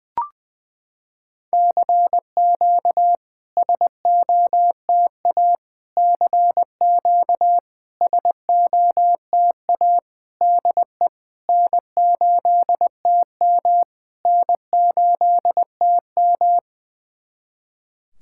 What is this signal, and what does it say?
Morse code, heard as a keyed tone, about 20 wpm, 700 Hz. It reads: CQ SOTA CQ SOTA DE N8TM N8TM
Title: SOTA QSO Practice 20wpm